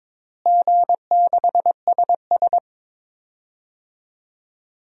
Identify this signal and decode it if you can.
Morse code, heard as a keyed tone, about 22 wpm, 700 Hz. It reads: G6SS
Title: DXCC-100 Call Signs - 4 Characters - Repeated 3x - 22wpm